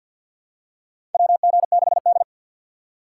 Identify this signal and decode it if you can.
Morse code, heard as a keyed tone, about 49 wpm, 700 Hz. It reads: WG6D
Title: USA Call Signs - 4 Characters - 6x Character Spacing - 50wpm